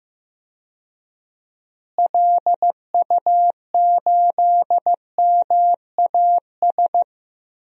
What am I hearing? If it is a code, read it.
Morse code, heard as a keyed tone, about 15 wpm, 700 Hz. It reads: LU8MAS
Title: DXCC-100 Call Signs - 6 Characters - Repeated 2x - 15wpm